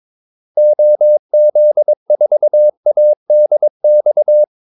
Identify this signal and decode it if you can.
Morse code, heard as a keyed tone, about 22 wpm, 600 Hz. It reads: OZ4ADX
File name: Callsigns-22wpm